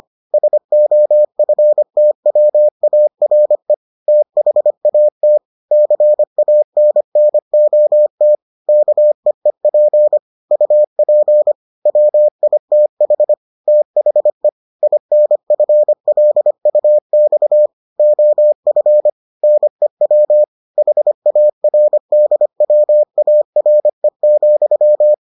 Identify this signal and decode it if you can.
Morse code, heard as a keyed tone, about 25 wpm, 600 Hz. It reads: SOFTWARE THAT CANNOT KEEP UP WITH THE INFLUX OF NEW HARDWARE,